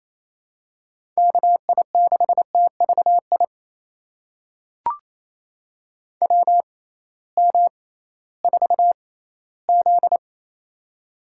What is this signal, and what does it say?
Morse code, heard as a keyed tone, about 28 wpm, spaced as for 16 wpm, 700 Hz. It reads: KI6TVI WM4Z
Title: USA Call Signs - 6x Character Spacing - 28wpm